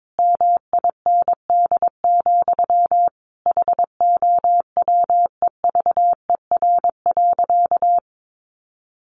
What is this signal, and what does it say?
Morse code, heard as a keyed tone, about 22 wpm, 700 Hz. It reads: MIND, HOWEVER.
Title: Sentences from Top 1100 Words 22wpm